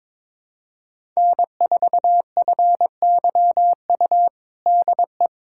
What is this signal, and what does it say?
Morse code, heard as a keyed tone, about 22 wpm, 700 Hz. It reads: N4FYU DE
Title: USA Callsign DE Callsign - 5 Characters - 22wpm